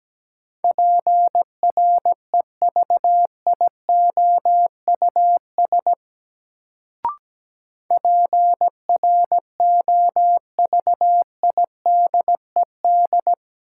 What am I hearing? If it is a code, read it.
Morse code, heard as a keyed tone, about 17 wpm, 700 Hz. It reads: PREVIOUS PROVIDED
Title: Sentences from Top 1500 Words - New Word Drill - 17wpm